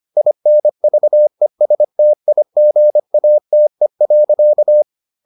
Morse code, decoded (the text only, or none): INVESTIGATE.